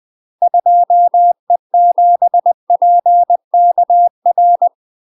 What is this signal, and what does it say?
Morse code, heard as a keyed tone, about 20 wpm, 700 Hz. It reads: 2E7PKR
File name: UK MASTER QSOs TRAINING / 20WPM / UK MASTER QSOs 0008